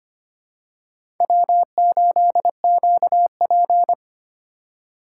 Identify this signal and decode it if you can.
Morse code, heard as a keyed tone, about 25 wpm, 700 Hz. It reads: W8QP
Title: USA Call Signs - 4 Characters - 4x Character Spacing - 25wpm